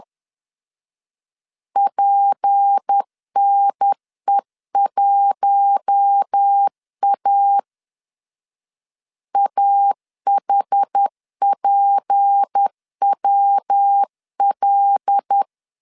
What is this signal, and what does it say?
Morse code, heard as a keyed tone, about 10 wpm, 800 Hz. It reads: PNE1A AHPWL
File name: LEZIONI / 6 PKX0 / 6 P 10wpm 2432